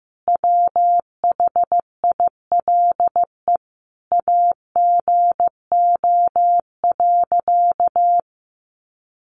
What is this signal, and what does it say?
Morse code, heard as a keyed tone, about 15 wpm, 700 Hz. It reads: WHILE AGO.
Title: Sentences from Top 600 Words 15wpm